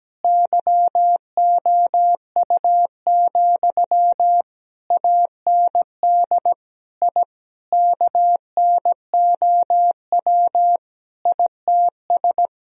Morse code, heard as a keyed tone, about 17 wpm, 700 Hz. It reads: YOU, AND I KNOW ITS
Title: Sentences from Top 1000 Words - Review Part 2 - 17wpm